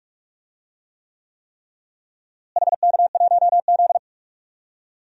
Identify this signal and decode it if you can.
Morse code, heard as a keyed tone, about 46 wpm, 700 Hz. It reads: HK1Z
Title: DXCC-100 Call Signs - 4 Characters - Repeated 3x - 45wpm